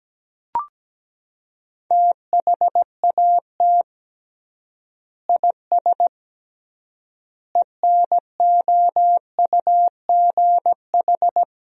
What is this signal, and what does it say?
Morse code, heard as a keyed tone, about 17 wpm, 700 Hz. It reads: THAT IS ENOUGH
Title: 3x Word-Spacing - Sentences from Top 400 Words - 17wpm